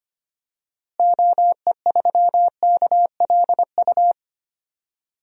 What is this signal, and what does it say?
Morse code, heard as a keyed tone, about 25 wpm, 700 Hz. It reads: OE3KLU
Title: DXCC-100 Call Signs - 6 Characters - Repeated 3x - 25wpm